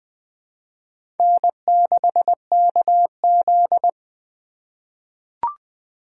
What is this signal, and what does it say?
Morse code, heard as a keyed tone, about 20 wpm, 700 Hz. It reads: N6KZ
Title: 2023 SOTA Chaser Call Signs - 20wpm